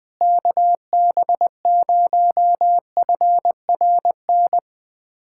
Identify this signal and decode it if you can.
Morse code, heard as a keyed tone, about 20 wpm, 700 Hz. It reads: KB0FRN
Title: USA Callsign DE Callsign - 6 Characters - 20wpm